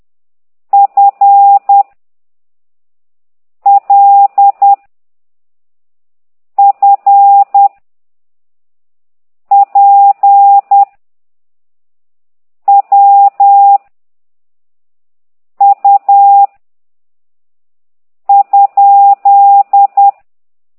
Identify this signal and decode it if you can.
Morse code, heard as a keyed tone, about 10 wpm, 800 Hz. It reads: F L F P W U ?
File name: ct2idw / exerc9